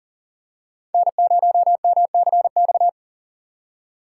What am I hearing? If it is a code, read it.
Morse code, heard as a keyed tone, about 40 wpm, 700 Hz. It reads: N0MCX
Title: USA Call Signs - 5 Characters - 3x Character Spacing - 40wpm